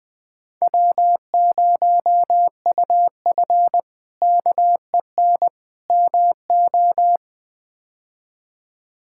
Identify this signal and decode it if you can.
Morse code, heard as a keyed tone, about 21 wpm, 700 Hz. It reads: W0UF KEN MO
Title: K1USN SST QSO Practice - 20wpm